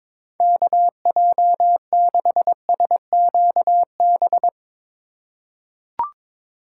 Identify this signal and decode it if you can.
Morse code, heard as a keyed tone, about 22 wpm, 700 Hz. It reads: KJ6SQB
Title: USA Call Signs - 6 Characters - 2x Character Spacing - 22wpm